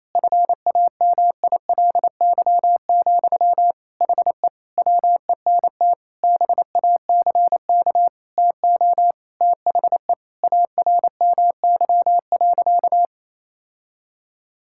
Morse code, heard as a keyed tone, about 28 wpm, 700 Hz. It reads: FAMILY, HE WENT BACK TO THE ARMY.
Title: Sentences from Top 1000 Words - Review Part 6 - 28wpm